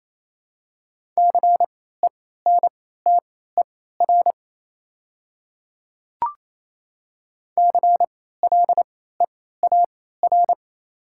Morse code, heard as a keyed tone, about 28 wpm, spaced as for 16 wpm, 700 Hz. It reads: CENTER CLEAR
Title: 3x Character Spacing - Top 500 Words - 28wpm